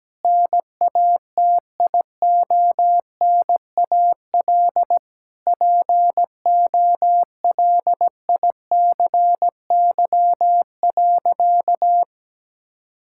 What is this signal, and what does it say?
Morse code, heard as a keyed tone, about 17 wpm, 700 Hz. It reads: NATIONAL POLICY.
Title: Sentences from Top 1000 Words - Review Part 6 - 17wpm